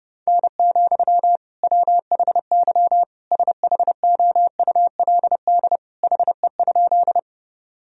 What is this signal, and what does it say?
Morse code, heard as a keyed tone, about 30 wpm, 700 Hz. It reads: N, WHY SHOULD HE?
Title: Sentences from Top 300 Words and Abbreviations 30wpm